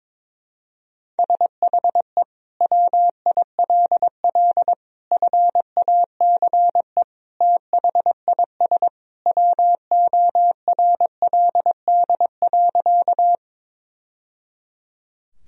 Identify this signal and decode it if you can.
Morse code, heard as a keyed tone, about 22 wpm, 700 Hz. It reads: SHE WILL FACE THIS WORLD.
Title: Sentences from Top 1000 Words - Review Part 2 - 22wpm